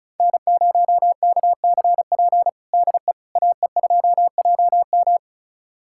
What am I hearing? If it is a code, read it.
Morse code, heard as a keyed tone, about 36 wpm, 690 Hz. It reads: N0KCP DE AE2JM
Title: USA Callsign DE Callsign 35wpm